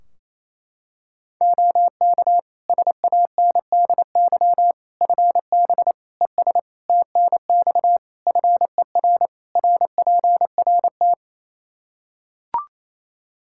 Morse code, heard as a keyed tone, about 28 wpm, 700 Hz. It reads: OK SANDY FB ES TNX FER RPRT
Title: K1USN SST QSO Practice 28wpm